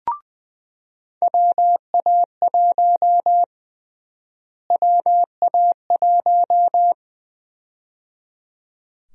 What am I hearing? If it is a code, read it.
Morse code, heard as a keyed tone, about 20 wpm, 700 Hz. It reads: WA1 WA1
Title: USA Call Sign Prefixes - Repeated 2x - 20wpm